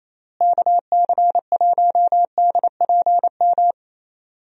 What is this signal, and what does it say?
Morse code, heard as a keyed tone, about 28 wpm, 700 Hz. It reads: KC1DPM
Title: USA Call Signs - 6 Characters - Repeated 3x - 28wpm